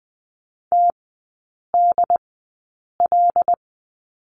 Morse code, heard as a keyed tone, about 20 wpm, spaced as for 8 wpm, 700 Hz. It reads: TDL
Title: 074 Lesson 13e - Letter L - Three Characters 20wpm